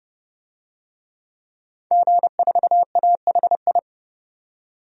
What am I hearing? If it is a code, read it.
Morse code, heard as a keyed tone, about 31 wpm, 700 Hz. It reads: G4AHI
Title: DXCC-100 Call Signs - 5 Characters - 30wpm